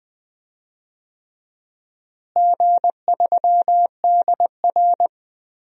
Morse code, heard as a keyed tone, about 20 wpm, 700 Hz. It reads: G3DR
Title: DXCC-100 Call Signs - 4 Characters - Repeated 3x - 20wpm